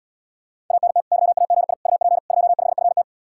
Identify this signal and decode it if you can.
Morse code, heard as a keyed tone, about 80 wpm, 700 Hz. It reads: ANT YAGI UP 25FT